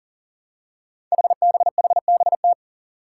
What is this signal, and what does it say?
Morse code, heard as a keyed tone, about 40 wpm, 700 Hz. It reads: HBHBT
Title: Five Challenging Letter-Number Combinations 40wpm